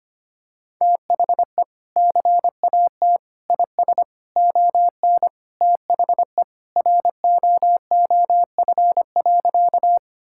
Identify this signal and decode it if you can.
Morse code, heard as a keyed tone, about 25 wpm, 700 Hz. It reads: THE CAT IS ON THE ROOF.
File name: practice / 25wpm / Sentences with 3-Word Phrases - 25wpm